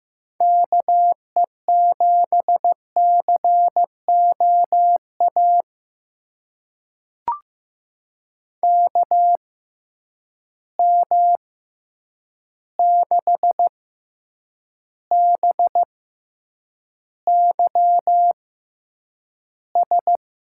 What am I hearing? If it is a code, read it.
Morse code, heard as a keyed tone, about 15 wpm, spaced as for 8 wpm, 700 Hz. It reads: KE7COA KM6BYS
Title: USA Call Signs - 6 Characters - 6x Character Spacing - 15wpm